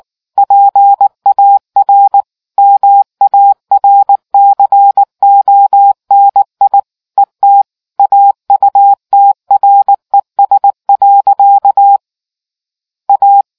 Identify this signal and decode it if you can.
Morse code, heard as a keyed tone, about 18 wpm, 800 Hz. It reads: PAR MARCONI ET AUTRES. A